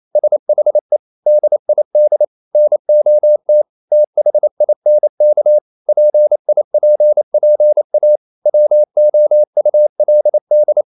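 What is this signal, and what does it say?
Morse code, heard as a keyed tone, about 28 wpm, 600 Hz. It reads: SHE DID NOT THINK PIPPA WOULD
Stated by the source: Morse code practice at 28 wpm, text of Ken Follett's novel Never in English, from CW med Gnister.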